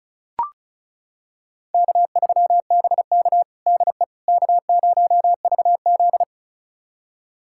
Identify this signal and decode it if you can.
Morse code, heard as a keyed tone, about 36 wpm, 700 Hz. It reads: K3BK DE K0VZ
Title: USA Callsign DE Callsign - 4 Characters - 35wpm